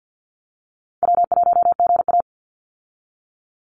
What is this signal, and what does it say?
Morse code, heard as a keyed tone, about 50 wpm, 700 Hz. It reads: W1GA
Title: USA Call Signs 50wpm